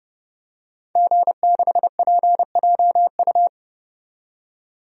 Morse code, heard as a keyed tone, about 30 wpm, 700 Hz. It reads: G6PJU
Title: DXCC-100 Call Signs - 5 Characters - 30wpm